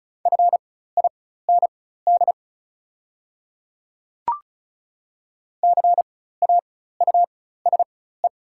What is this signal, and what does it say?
Morse code, heard as a keyed tone, about 36 wpm, spaced as for 16 wpm, 700 Hz. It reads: FIND CAUSE
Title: 4x Character Spacing - Top 200 Words - 35wpm